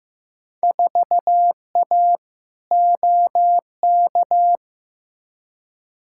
Morse code, heard as a keyed tone, about 15 wpm, 700 Hz. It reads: 4A OK
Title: Field Day Exchanges - 15wpm